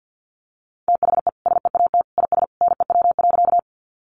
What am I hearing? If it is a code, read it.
Morse code, heard as a keyed tone, about 50 wpm, 700 Hz. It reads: THE SEAT IS NEW.
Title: Sentences from Top 1500 Words 50wpm